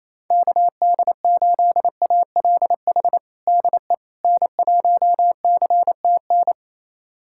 Morse code, heard as a keyed tone, about 28 wpm, 700 Hz. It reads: KD8ALH DE N1CTN
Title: USA Callsign DE Callsign 28wpm